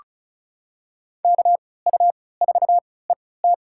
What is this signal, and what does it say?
Morse code, heard as a keyed tone, about 36 wpm, spaced as for 20 wpm, 700 Hz. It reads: KU4ET
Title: USA Call Signs - 5 Characters - 3x Character Spacing - 35wpm